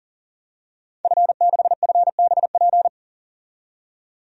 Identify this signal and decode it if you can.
Morse code, heard as a keyed tone, about 40 wpm, 700 Hz. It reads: F6FBP
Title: DXCC-100 Call Signs - 5 Characters - 40wpm